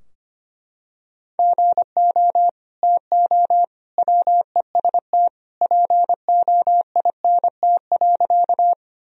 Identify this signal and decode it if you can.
Morse code, heard as a keyed tone, about 25 wpm, 700 Hz. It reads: GO TO WEST POINT.
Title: Sentences from Top 1000 Words - Review Part 2 - 25wpm